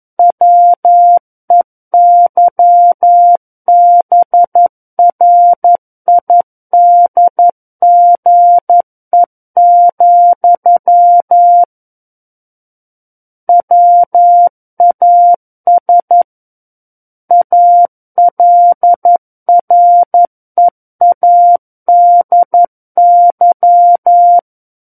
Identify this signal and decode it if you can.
Morse code, heard as a keyed tone, about 11 wpm, 700 Hz. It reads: WEYBRIDGE, WAS ALREADY
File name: Book One / 02-Chapter 2 - 11 WPM